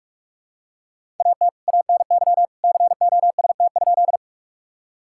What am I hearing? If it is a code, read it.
Morse code, heard as a keyed tone, about 46 wpm, 700 Hz. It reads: AT ANY COST?
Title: Sentences from Top 1000 Words - Review Part 5 - 45wpm